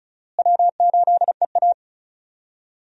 Morse code, heard as a keyed tone, about 34 wpm, 700 Hz. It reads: W8EA
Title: USA Call Signs - 4 Characters - Repeated 3x - 35wpm